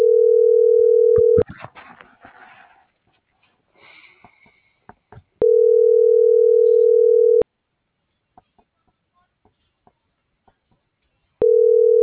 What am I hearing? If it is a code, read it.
Ringback tone (North America), 440 + 480 Hz.